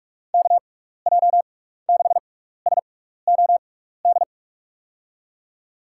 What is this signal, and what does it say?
Morse code, heard as a keyed tone, about 44 wpm, spaced as for 15 wpm, 700 Hz. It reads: KJ6SOD
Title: USA Call Signs - 6 Characters - 6x Character Spacing - 45wpm